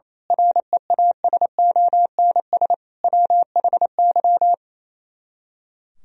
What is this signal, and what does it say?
Morse code, heard as a keyed tone, about 26 wpm, 700 Hz. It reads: REASONS WHY
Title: Speed-Racing - Sets of 5 Words 28wpm